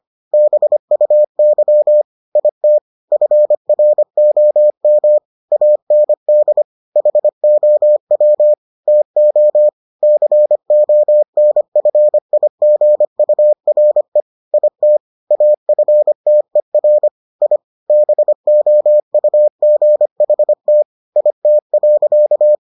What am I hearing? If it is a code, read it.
Morse code, heard as a keyed tone, about 25 wpm, 600 Hz. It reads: BUY IT FROM AND HOW TO CONFIGURE IT AFTER I BOUGHT IT.